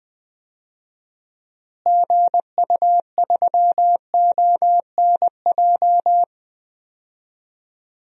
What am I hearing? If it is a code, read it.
Morse code, heard as a keyed tone, about 20 wpm, 700 Hz. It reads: GU3ONJ
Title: DXCC-100 Call Signs - 6 Characters - Repeated 2x - 20wpm